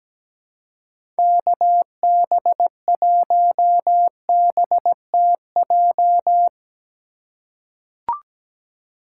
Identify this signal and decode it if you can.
Morse code, heard as a keyed tone, about 17 wpm, 700 Hz. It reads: KB1BTJ
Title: USA Call Signs - 6 Characters - 3x Character Spacing - 17wpm